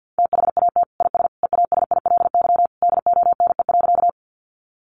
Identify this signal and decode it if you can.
Morse code, heard as a keyed tone, about 49 wpm, 700 Hz. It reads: THAT IS EASILY DONE.
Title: Sentences from Top 900 Words 50wpm